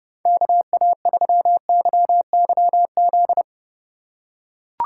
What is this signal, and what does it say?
Morse code, heard as a keyed tone, about 30 wpm, 700 Hz. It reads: KA3YYZ
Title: USA Call Signs - 3x Character Spacing - 30wpm